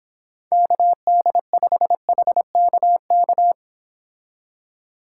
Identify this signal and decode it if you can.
Morse code, heard as a keyed tone, about 30 wpm, 700 Hz. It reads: KD5HKK